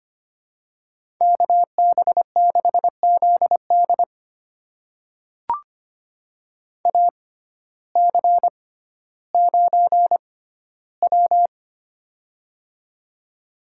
Morse code, heard as a keyed tone, about 25 wpm, spaced as for 14 wpm, 700 Hz. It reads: KB6ZD AC9W
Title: USA Call Signs - 6x Character Spacing - 25wpm